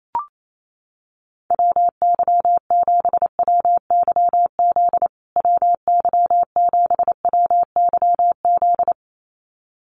Morse code, heard as a keyed tone, about 28 wpm, 700 Hz. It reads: WY7WYZ WY7WYZ
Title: USA Call Signs Repeated 2x 28wpm